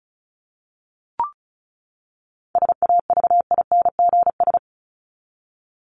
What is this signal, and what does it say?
Morse code, heard as a keyed tone, about 34 wpm, 700 Hz. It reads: SAVINGS
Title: Top 1001-1250 Words 35wpm